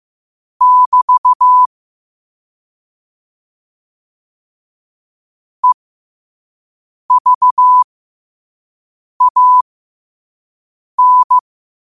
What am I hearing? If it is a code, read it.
Morse code, heard as a keyed tone, about 15 wpm, spaced as for 5 wpm, 1000 Hz. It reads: <BT> EVAN
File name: qotdfiles / 5wpm